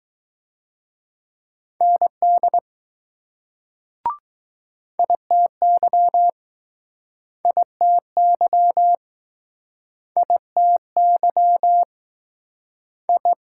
Morse code, heard as a keyed tone, about 19 wpm, 700 Hz. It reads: ND ITY ITY ITY I